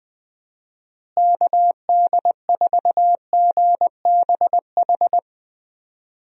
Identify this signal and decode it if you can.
Morse code, heard as a keyed tone, about 20 wpm, 700 Hz. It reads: KD4GBH
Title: USA Call Signs - 6 Characters - Repeated 2x - 20wpm